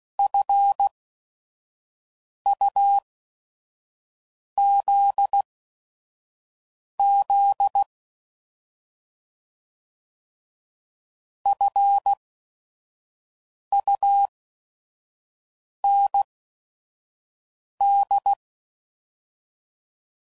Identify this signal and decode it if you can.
Morse code, heard as a keyed tone, about 16 wpm, spaced as for 5 wpm, 800 Hz. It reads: FUZZ FUND
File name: Prefixes / By First Letter / Fs